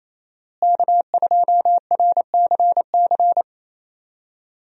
Morse code, heard as a keyed tone, about 28 wpm, 700 Hz. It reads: K2RCC
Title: USA Call Signs - 5 Characters - Repeated 2x - 28wpm